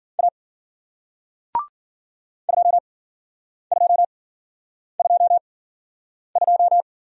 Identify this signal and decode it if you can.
Morse code, heard as a keyed tone, about 51 wpm, spaced as for 9 wpm, 700 Hz. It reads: A 2222